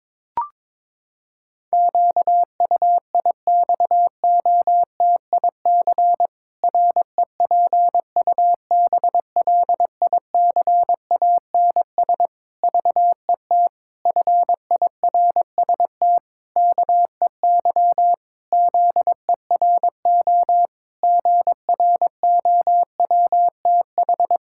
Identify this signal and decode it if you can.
Morse code, heard as a keyed tone, about 22 wpm, 700 Hz. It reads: QUIXOTIC REPUBLICANS VET FIRST KEY ZERO GROWTH
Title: Pangram Challenge v2 - 22wpm